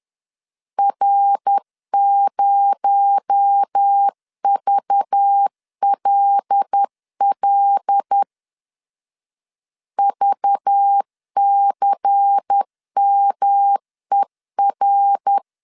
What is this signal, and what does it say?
Morse code, heard as a keyed tone, about 10 wpm, 800 Hz. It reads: R0VLL VCMER
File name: CW / DET / 1 6 1500C S B 10wpm 4137